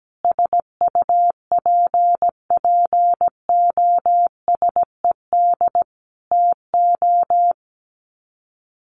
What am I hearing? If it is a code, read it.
Morse code, heard as a keyed tone, about 17 wpm, 700 Hz. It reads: SUPPOSED TO
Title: Sets of 5 Words Encore2 17wpm